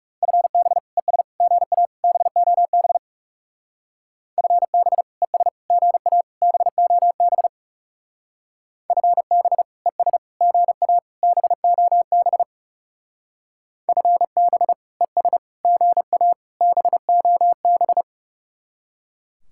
Morse code, heard as a keyed tone, about 36 wpm, 700 Hz. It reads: FB ES GA BOB FB ES GA BOB FB ES GA BOB FB ES GA BOB
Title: Speed-Racing Common QSO Phrases Encore - 30wpm